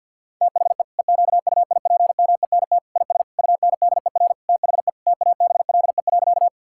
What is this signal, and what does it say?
Morse code, heard as a keyed tone, about 50 wpm, 700 Hz. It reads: THE EQUIPMENT IS UNDER THE TABLE.
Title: Sentences from Top 1000 Words - Review Part 5 - 50wpm